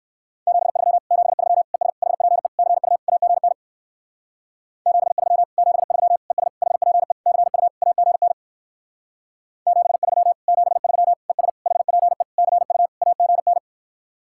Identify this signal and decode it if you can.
Morse code, heard as a keyed tone, about 57 wpm, 700 Hz. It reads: 73 73 ES HPE CU AGN 73 73 ES HPE CU AGN 73 73 ES HPE CU AGN